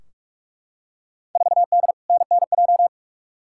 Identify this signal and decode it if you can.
Morse code, heard as a keyed tone, about 43 wpm, 700 Hz. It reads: 4D NNJ
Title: Field Day Exchanges Encore - 45wpm